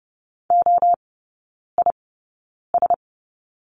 Morse code, heard as a keyed tone, about 30 wpm, 700 Hz. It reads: O I S
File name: Lesson 19e - Letter w - Three Characters 30wpm